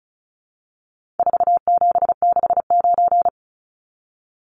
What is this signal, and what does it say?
Morse code, heard as a keyed tone, about 34 wpm, 700 Hz. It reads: 4769
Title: Numbers - 4-Digit 35wpm